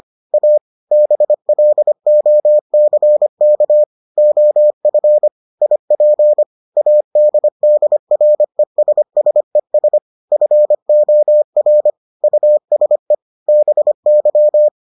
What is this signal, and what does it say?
Morse code, heard as a keyed tone, about 25 wpm, 600 Hz. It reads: A BLOCK OF IP ADDRESSES FOR USE BY